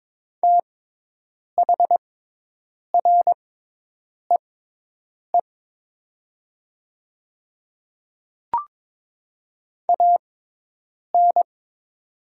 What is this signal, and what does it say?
Morse code, heard as a keyed tone, about 22 wpm, spaced as for 8 wpm, 700 Hz. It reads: THREE AN
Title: 6x Character Spacing - Top 200 Words - 22wpm